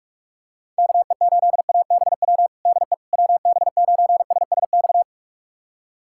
Morse code, heard as a keyed tone, about 41 wpm, 700 Hz. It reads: KE8ABW DE WB9SSX
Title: USA Callsign DE Callsign - 6 Characters - 45wpm